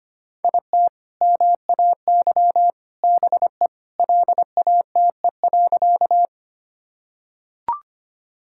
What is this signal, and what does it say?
Morse code, heard as a keyed tone, about 25 wpm, 700 Hz. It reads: IT MAY BE LATE.
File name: practice / 25wpm / Sentences with 2-Word Phrases - 25wpm